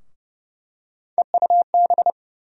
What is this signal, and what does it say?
Morse code, heard as a keyed tone, about 30 wpm, 700 Hz. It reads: EUB
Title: USA Call Sign Postfixes - 3 Letters 30wpm